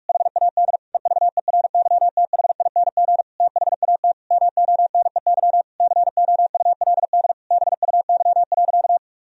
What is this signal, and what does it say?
Morse code, heard as a keyed tone, about 46 wpm, 700 Hz. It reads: HAD EVERYTHING THAT MONEY COULD BUY.
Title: Sentences from Top 1000 Words - Review Part 4 - 45wpm